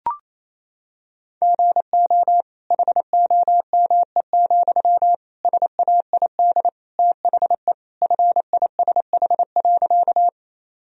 Morse code, heard as a keyed tone, about 28 wpm, 700 Hz. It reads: GO HOME, SAID THE FISH.
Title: Sentences from Top 1000 Words - Review Part 1 - 28wpm